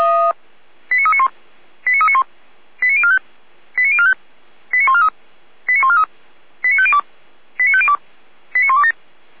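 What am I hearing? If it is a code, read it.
Five-tone selective call (ZVEI), heard as quick runs of five tones, tones of about 80 ms. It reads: 89391 89391 89045 89045 89143 89143 89502 89502 89127
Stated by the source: ZVEI five-tone selective call received by SDR.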